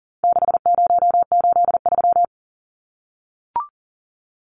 Morse code, heard as a keyed tone, about 40 wpm, 700 Hz. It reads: 6083
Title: Numbers - 4-Digit 40wpm